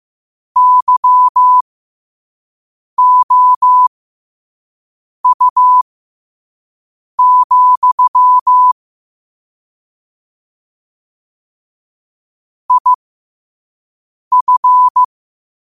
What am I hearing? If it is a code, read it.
Morse code, heard as a keyed tone, about 15 wpm, spaced as for 5 wpm, 1000 Hz. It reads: YOU, IF
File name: qotdfiles / 5wpm